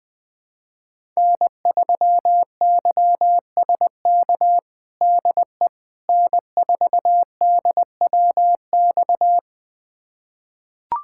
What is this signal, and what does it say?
Morse code, heard as a keyed tone, about 20 wpm, 700 Hz. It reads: N3YSK DE N4DWX
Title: USA Callsign DE Callsign - 5 Characters - 20wpm